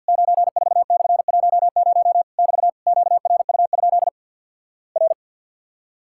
Morse code, heard as a keyed tone, about 48 wpm, 700 Hz. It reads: 94/10 <BT> QRU? R
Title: Worked All Europe DX Contest - QTC Exchange - 50wpm